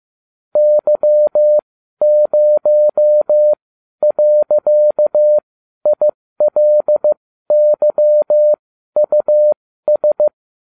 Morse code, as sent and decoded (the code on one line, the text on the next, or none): -.-- ----- .-.-.- .. .-.. -.-- ..- ...
Y0.ILYUS